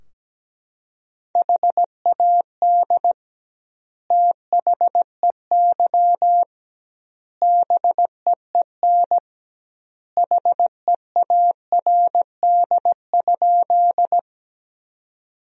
Morse code, heard as a keyed tone, about 17 wpm, 700 Hz. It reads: HAD THEY BEEN HEARD?
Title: 2x Word-Spacing - Sentences from Top 400 Words - 17wpm